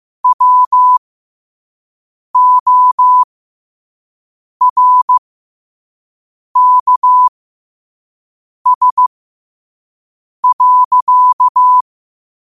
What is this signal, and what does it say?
Morse code, heard as a keyed tone, about 15 wpm, spaced as for 5 wpm, 1000 Hz. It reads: WORKS.
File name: qotdfiles / 5wpm